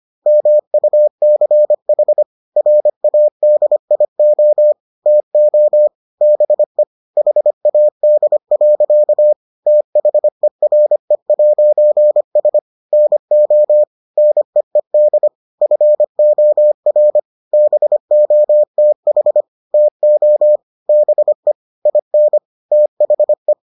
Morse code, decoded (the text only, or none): MUCH RADIO TO BE HAD. THERE'S NO NEED FOR BOTH TO BE IN THE